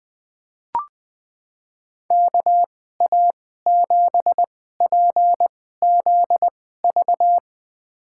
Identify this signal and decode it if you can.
Morse code, heard as a keyed tone, about 20 wpm, spaced as for 14 wpm, 700 Hz. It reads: KA7PZV
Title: USA Call Signs - 6 Characters - 2x Character Spacing - 20wpm